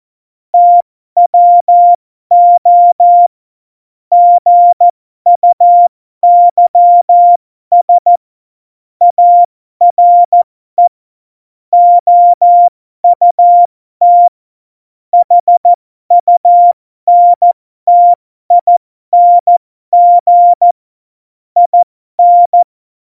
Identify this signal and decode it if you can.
Morse code, heard as a keyed tone, about 14 wpm, spaced as for 12 wpm, 700 Hz. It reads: TWO GUYS ARE OUT HUNTING IN
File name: Morsekurs / Witze